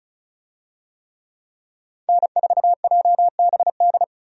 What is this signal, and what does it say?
Morse code, heard as a keyed tone, about 36 wpm, 700 Hz. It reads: N4JBD
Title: USA Call Signs - 5 Characters - 35wpm